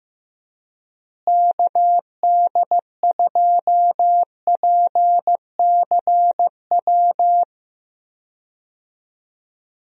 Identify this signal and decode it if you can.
Morse code, heard as a keyed tone, about 15 wpm, 700 Hz. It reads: KD2PCW
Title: USA Call Signs - 6 Characters - Repeated 2x - 15wpm